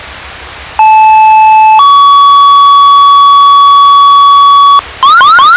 Two-tone page, heard as a one-second tone, then a three-second tone, about 870 then 1130 Hz, with noise. It is heard from 0.8 to 4.8 s.